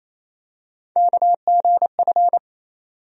Morse code, heard as a keyed tone, about 27 wpm, 700 Hz. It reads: KGF
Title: USA Call Sign Postfixes - 3 Letters 28wpm